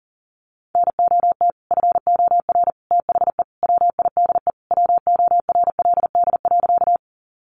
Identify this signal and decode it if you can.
Morse code, heard as a keyed tone, about 40 wpm, 700 Hz. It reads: NOT FOR THE WIDE WORLD.
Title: Sentences from Top 900 Words 40wpm